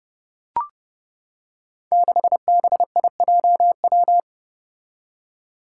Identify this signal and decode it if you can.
Morse code, heard as a keyed tone, about 30 wpm, 700 Hz. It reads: 6BIJW
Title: Five Challenging Letter-Number Combinations 30wpm